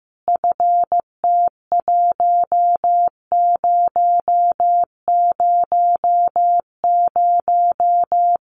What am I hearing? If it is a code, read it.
Morse code, heard as a keyed tone, about 15 wpm, 700 Hz. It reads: FT1000
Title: Common QSO Phrases 15wpm